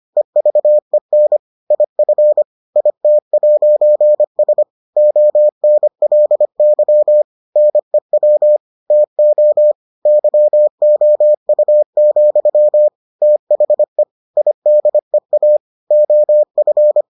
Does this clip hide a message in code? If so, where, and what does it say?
Yes: Morse code from 0.2 to 17.0 s, heard as a keyed tone, about 25 wpm, 600 Hz. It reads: EVEN IF IT'S ONLY NEW TO YOU, THE IDEA OF